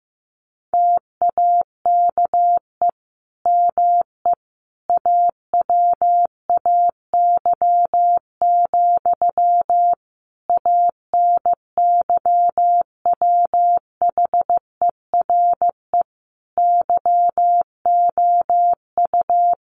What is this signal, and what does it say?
Morse code, heard as a keyed tone, about 15 wpm, 700 Hz. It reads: TAKE ME AWAY, ANYWHERE YOU
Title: Sentences from Top 1400 Words 15wpm